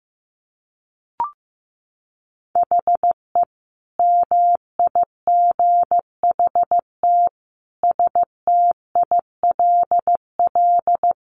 Morse code, heard as a keyed tone, about 15 wpm, 700 Hz. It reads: HE MIGHT STILL
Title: Sentences from Top 300 Words 15wpm